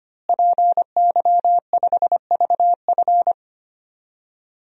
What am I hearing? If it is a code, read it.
Morse code, heard as a keyed tone, about 25 wpm, 700 Hz. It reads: PY5VF
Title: DXCC-100 Call Signs - 5 Characters - Repeated 3x - 25wpm